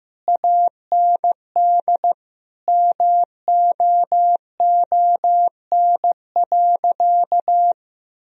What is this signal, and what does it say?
Morse code, heard as a keyed tone, about 15 wpm, 700 Hz. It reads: AND MOON.
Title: Sentences from Top 1000 Words - Review Part 3 - 15wpm